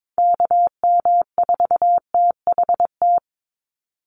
Morse code, heard as a keyed tone, about 22 wpm, 700 Hz. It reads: KM4THT
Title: USA Call Signs 22wpm